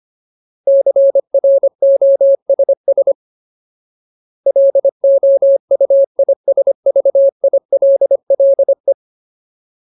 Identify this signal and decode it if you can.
Morse code, heard as a keyed tone, about 25 wpm, 550 Hz. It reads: CROSS LOUISVILLE
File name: Ultimate Word List / 25WPM Ultimate